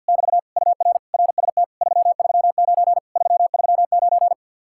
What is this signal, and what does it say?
Morse code, heard as a keyed tone, about 65 wpm, spaced as for 57 wpm, 700 Hz. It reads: <BT> UR RST 339 339
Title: Speed-Racing Common QSO Phrases - 50wpm